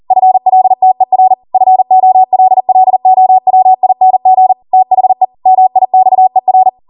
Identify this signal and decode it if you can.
Morse code, heard as a keyed tone, about 40 wpm, 750 Hz. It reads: FLTER FOLLOWING THE MIXER
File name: Morse / 40WPM